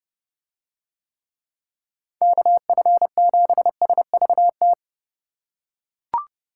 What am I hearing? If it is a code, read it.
Morse code, heard as a keyed tone, about 29 wpm, 700 Hz. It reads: KF7SVT